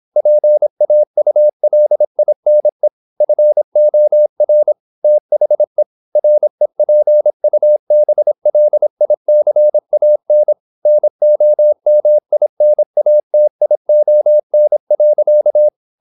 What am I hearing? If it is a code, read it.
Morse code, heard as a keyed tone, about 26 wpm, 600 Hz. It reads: PAULINE FOR THE REPUBLICAN NOMINATION.
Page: Never 26wpm | CW med Gnister